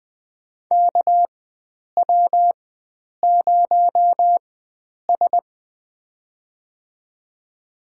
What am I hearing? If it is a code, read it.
Morse code, heard as a keyed tone, about 20 wpm, spaced as for 9 wpm, 700 Hz. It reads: KW0S
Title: USA Call Signs - 4 Characters - 4x Character Spacing - 20wpm